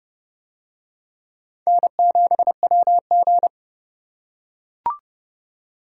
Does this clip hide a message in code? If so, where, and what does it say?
Yes: Morse code from 1.7 to 3.5 s, heard as a keyed tone, about 28 wpm, 700 Hz. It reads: N7WG